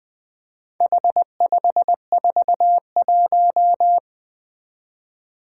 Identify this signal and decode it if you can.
Morse code, heard as a keyed tone, about 20 wpm, 700 Hz. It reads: H541
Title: Four Challenging Letter-Number Combinations 20wpm